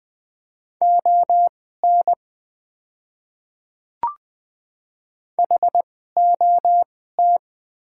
Morse code, heard as a keyed tone, about 20 wpm, spaced as for 15 wpm, 700 Hz. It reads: ON HOT
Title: 2x Character Spacing - Top 500 Words - 20wpm